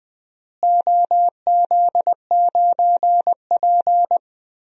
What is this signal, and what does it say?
Morse code, heard as a keyed tone, about 20 wpm, 700 Hz. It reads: OZ9P
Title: DXCC-100 Call Signs - 4 Characters - Repeated 3x - 20wpm